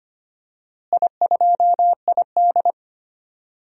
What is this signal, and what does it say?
Morse code, heard as a keyed tone, about 25 wpm, 700 Hz. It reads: I2ID
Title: DXCC-100 Call Signs - 4 Characters - Repeated 2x - 25wpm